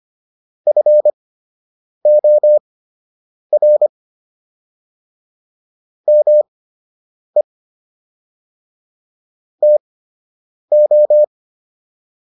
Morse code, heard as a keyed tone, about 25 wpm, spaced as for 8 wpm, 600 Hz. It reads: FOR ME TO